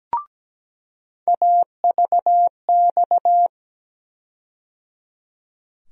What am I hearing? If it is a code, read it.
Morse code, heard as a keyed tone, about 17 wpm, 700 Hz. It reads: AVX
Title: USA Call Sign Postfixes - 3 Letters 17wpm